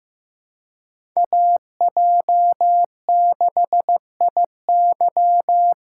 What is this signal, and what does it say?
Morse code, heard as a keyed tone, about 15 wpm, 700 Hz. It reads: AJ6IY
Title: USA Call Signs - 5 Characters - 2x Character Spacing - 15wpm